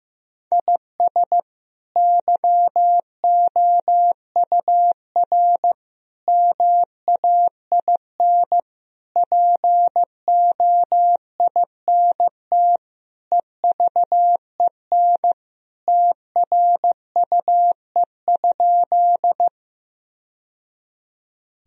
Morse code, heard as a keyed tone, about 15 wpm, 700 Hz. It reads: IS YOUR MAIN POINT EVEN TRUE?
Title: Sentences from Top 1000 Words - Review Part 2 - 15wpm